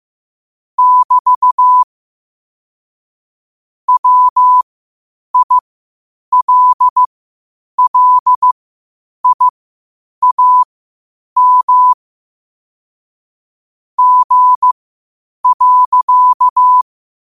Morse code, heard as a keyed tone, about 15 wpm, spaced as for 8 wpm, 1000 Hz. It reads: <BT> WILLIAM G.
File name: qotdfiles / 8wpm